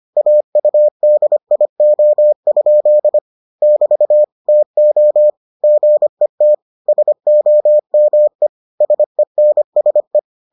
Morse code, heard as a keyed tone, about 25 wpm, 600 Hz. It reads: AUDIO? <BT> TO GET SOME SENSE